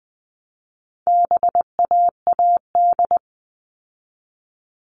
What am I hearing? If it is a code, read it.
Morse code, heard as a keyed tone, about 20 wpm, 700 Hz. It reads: BAAD
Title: All Cut Numbers - 4-Digit 20wpm